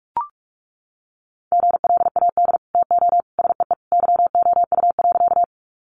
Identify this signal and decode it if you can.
Morse code, heard as a keyed tone, about 44 wpm, 700 Hz. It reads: GLAD TO SEE YOU.
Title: Sentences from Top 800 Words 45wpm